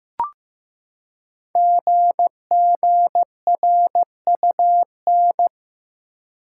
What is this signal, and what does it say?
Morse code, heard as a keyed tone, about 15 wpm, 700 Hz. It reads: GGRUN
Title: Code Groups - 5 Letters - 15wpm